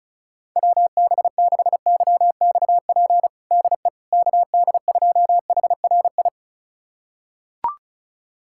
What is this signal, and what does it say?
Morse code, heard as a keyed tone, about 36 wpm, 700 Hz. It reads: WB6YXP DE KD2HRI
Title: USA Callsign DE Callsign 35wpm